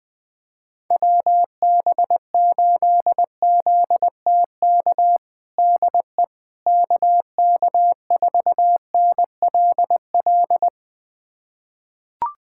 Morse code, heard as a keyed tone, about 20 wpm, 700 Hz. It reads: WB8ZTK DE KK4NLL
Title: USA Callsign DE Callsign - 6 Characters - 20wpm